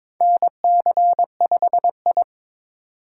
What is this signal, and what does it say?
Morse code, heard as a keyed tone, about 23 wpm, 700 Hz. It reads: NC5I
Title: USA Call Signs - 4 Characters - 4x Character Spacing - 22wpm